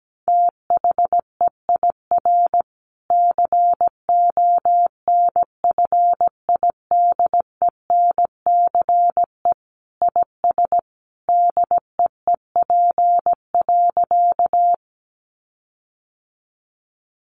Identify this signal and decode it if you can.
Morse code, heard as a keyed tone, about 17 wpm, 700 Hz. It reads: THEIR CONFIDENCE IS DEEP.
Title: Sentences from Top 1500 Words 17wpm